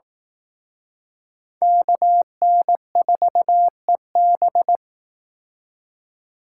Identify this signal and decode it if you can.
Morse code, heard as a keyed tone, about 19 wpm, 700 Hz. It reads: KN4EB
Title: Speed Racing - USA 2x2 Call Signs 15wpm